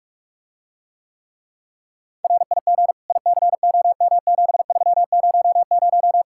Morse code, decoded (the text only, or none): RIG ICOM7300